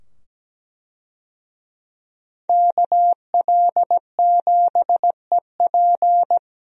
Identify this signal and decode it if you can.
Morse code, heard as a keyed tone, about 17 wpm, 700 Hz. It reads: KL7EP
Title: DXCC-100 Call Signs - 5 Characters - 17wpm